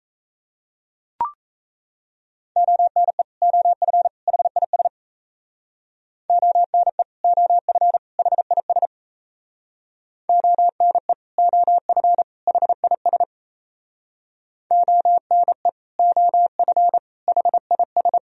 Morse code, heard as a keyed tone, about 35 wpm, 700 Hz. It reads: ONE OF HIS ONE OF HIS ONE OF HIS ONE OF HIS